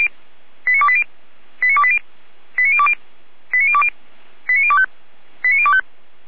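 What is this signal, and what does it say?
Five-tone selective call (ZVEI), heard as quick runs of five tones, tones of about 70 ms. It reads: 89280 89280 89020 89020 89025 89025
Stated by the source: ZVEI five-tone selective call 89262 received by SDR, fire brigade feuerwehr-orlamuende.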